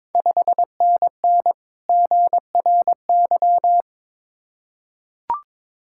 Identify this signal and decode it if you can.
Morse code, heard as a keyed tone, about 22 wpm, 690 Hz. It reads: 5NN GRY
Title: Ontario QSO Party Exchange 22wpm